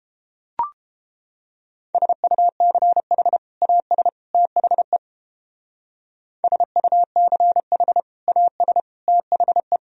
Morse code, heard as a keyed tone, about 30 wpm, 700 Hz. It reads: SUCH AS THE SUCH AS THE